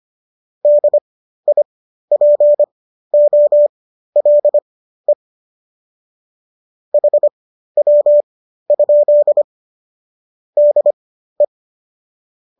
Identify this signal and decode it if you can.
Morse code, heard as a keyed tone, about 25 wpm, spaced as for 13 wpm, 590 Hz. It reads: DIPOLE HW? DE
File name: qso211 13